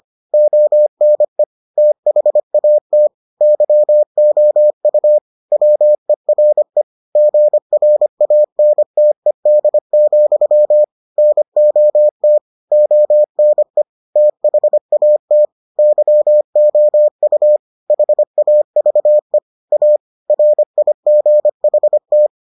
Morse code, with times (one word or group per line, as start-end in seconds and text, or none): 0.3-1.4 ONE
1.8-3.1 THAT
3.4-5.2 YOU
5.5-6.8 WERE
7.2-10.8 GRANTED,
11.2-12.4 NOT
12.7-13.8 ONE
14.2-15.4 THAT
15.8-17.6 YOU
17.9-19.4 HAVE
19.7-20.0 A
20.3-22.3 RIGHT